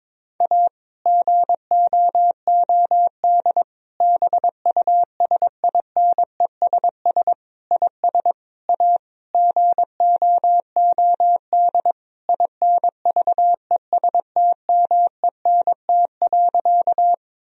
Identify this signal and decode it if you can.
Morse code, heard as a keyed tone, about 22 wpm, 700 Hz. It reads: A GOOD BUSINESS IS A GOOD INVESTMENT.
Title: Sentences from Top 1000 Words - Review Part 4 - 22wpm